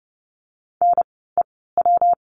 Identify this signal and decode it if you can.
Morse code, heard as a keyed tone, about 30 wpm, spaced as for 17 wpm, 700 Hz. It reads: NEW
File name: course / 30:10wpm / 237 Lesson 38f - Letter q - Words 30wpm